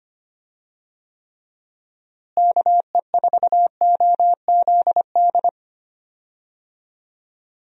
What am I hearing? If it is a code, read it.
Morse code, heard as a keyed tone, about 25 wpm, 700 Hz. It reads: KE4OZD
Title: USA Call Signs - 6 Characters - Repeated 3x - 25wpm